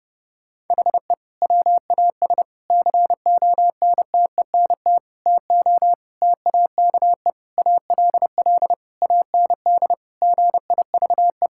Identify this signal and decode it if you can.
Morse code, heard as a keyed tone, about 32 wpm, 700 Hz. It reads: HE WAS CONTENT TO TAKE ALL AND GIVE
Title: Sentences from Top 1000 Words - Review Part 6 - 30wpm